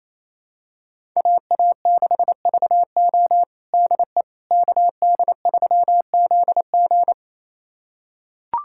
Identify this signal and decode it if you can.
Morse code, heard as a keyed tone, about 28 wpm, 700 Hz. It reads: AA6VO DE KD3ZG
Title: USA Callsign DE Callsign - 5 Characters - 28wpm